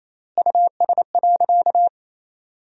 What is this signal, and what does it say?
Morse code, heard as a keyed tone, about 28 wpm, 700 Hz. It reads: US.
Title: 3x Word-Spacing - Sentences from Top 500 Words - 28wpm